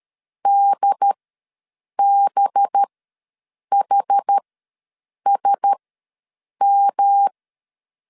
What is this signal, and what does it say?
Morse code, heard as a keyed tone, about 13 wpm, spaced as for 7 wpm, 800 Hz. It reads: DBHSM
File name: LEZIONI / 5 LBVC / 5 LB ws 12wpm 3034